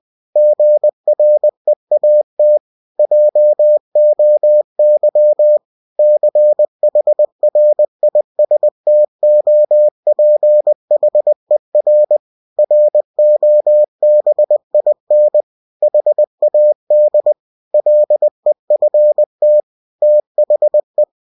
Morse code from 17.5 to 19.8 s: LEFT